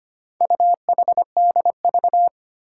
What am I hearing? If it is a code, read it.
Morse code, heard as a keyed tone, about 25 wpm, 700 Hz. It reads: UHDV
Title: Four Challenging Letter-Number Combinations 25wpm